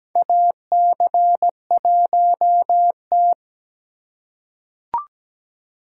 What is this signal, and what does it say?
Morse code, heard as a keyed tone, about 17 wpm, 700 Hz. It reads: AC1T
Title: USA Call Signs - 2x Character Spacing - 17wpm